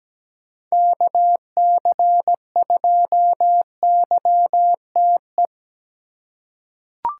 Morse code, heard as a keyed tone, about 17 wpm, 700 Hz. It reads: KC2YTE
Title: USA Call Signs - 3x Character Spacing - 17wpm